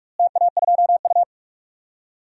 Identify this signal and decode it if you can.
Morse code, heard as a keyed tone, about 46 wpm, 700 Hz. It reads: TA2U
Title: DXCC-100 Call Signs - 4 Characters - Repeated 3x - 45wpm